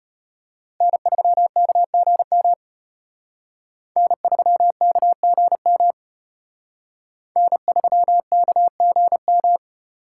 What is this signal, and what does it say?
Morse code, heard as a keyed tone, about 31 wpm, 700 Hz. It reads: N3KGM N3KGM N3KGM